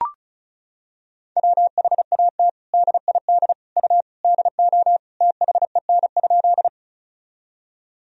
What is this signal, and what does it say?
Morse code, heard as a keyed tone, about 36 wpm, 700 Hz. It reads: WHAT DID U DO THEN?
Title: Sentences from Top 100 Words and Abbreviations 35wpm